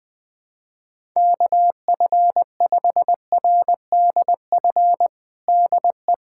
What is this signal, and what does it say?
Morse code, heard as a keyed tone, about 20 wpm, 700 Hz. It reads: KF5RDF DE
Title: USA Callsign DE Callsign - 6 Characters - 20wpm